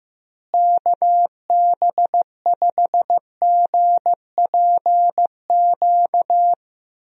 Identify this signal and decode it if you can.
Morse code, heard as a keyed tone, about 15 wpm, 700 Hz. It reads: KB5GPQ
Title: USA Call Signs - 6 Characters - Repeated 3x - 15wpm